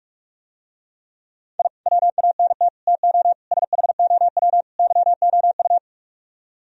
Morse code, heard as a keyed tone, about 45 wpm, 700 Hz. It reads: I WANT TO SHOW YOU